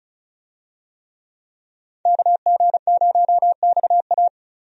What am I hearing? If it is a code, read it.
Morse code, heard as a keyed tone, about 34 wpm, 700 Hz. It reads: KG0XA
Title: USA Call Signs - 5 Characters - 35wpm